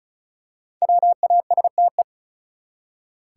Morse code, heard as a keyed tone, about 36 wpm, 700 Hz. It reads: WASTE
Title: Five Letter Words - 35wpm